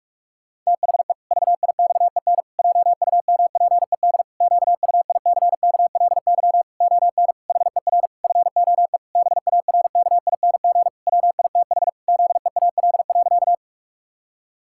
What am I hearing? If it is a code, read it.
Morse code, heard as a keyed tone, about 41 wpm, 700 Hz. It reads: THE VIXEN JUMPED QUICKLY ON HER FOE BARKING WITH ZEAL.
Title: Pangram Challenge v2 - 45wpm